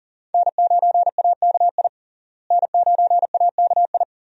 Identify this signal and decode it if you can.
Morse code, heard as a keyed tone, about 40 wpm, 700 Hz. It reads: N9AKI N9AKI
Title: USA Call Signs - 5 Characters - Repeated 3x - 40wpm